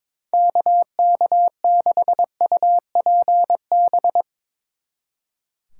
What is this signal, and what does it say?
Morse code, heard as a keyed tone, about 22 wpm, 700 Hz. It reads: KK6UPB
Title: Speed Racing - USA 2x3 Call Signs 22wpm